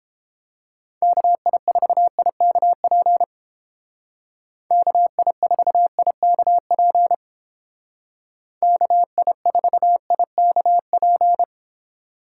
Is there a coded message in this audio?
Yes: Morse code from 1.0 to 11.4 s, heard as a keyed tone, about 29 wpm, 700 Hz. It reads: KI4IKP KI4IKP KI4IKP